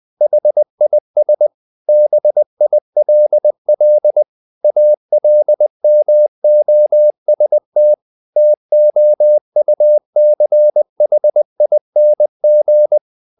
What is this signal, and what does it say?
Morse code, heard as a keyed tone, about 20 wpm, 600 Hz. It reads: HIS BILL ALMOST TOUCHING
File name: the old man and the sea part 2 20wpm